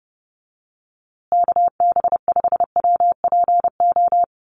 Morse code, heard as a keyed tone, about 30 wpm, 700 Hz. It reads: KB5WPO
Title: USA Call Signs Repeated 2x 30wpm